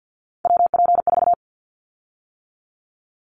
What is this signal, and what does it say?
Morse code, heard as a keyed tone, about 50 wpm, 700 Hz. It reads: WP4
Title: Callsign Prefixes for Top 25 DXCC - 50wpm